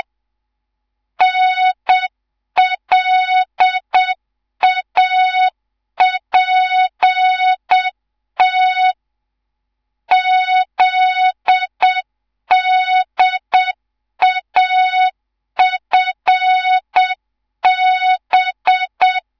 Morse code, heard as a keyed tone, about 7 wpm, 3700 Hz. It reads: NLAPT ZDAFB
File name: Morze-HA2MN / hang / 11 GYAKORLAT